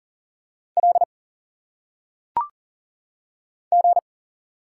Morse code, heard as a keyed tone, about 40 wpm, 700 Hz. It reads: L G
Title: Single Letters - Phonetic Pronunciation 40wpm